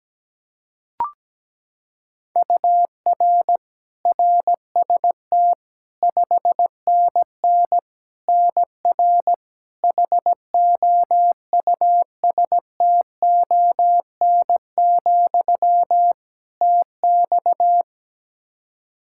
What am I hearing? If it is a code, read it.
Morse code, heard as a keyed tone, about 17 wpm, 700 Hz. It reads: UR RST 5NN NR HOUSTON, TX
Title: K1USN SST QSO Practice 17wpm